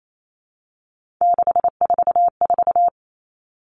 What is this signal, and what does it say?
Morse code, heard as a keyed tone, about 28 wpm, 700 Hz. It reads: 644
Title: Numbers - 3-Digit 28wpm